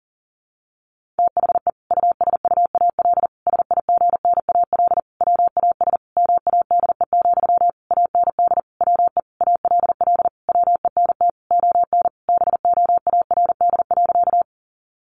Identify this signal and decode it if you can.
Morse code, heard as a keyed tone, about 40 wpm, 700 Hz. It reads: THE USUAL SIGNAL WAS MADE, AND WE ALL WENT ON BOARD.
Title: Sentences from Top 1400 Words 40wpm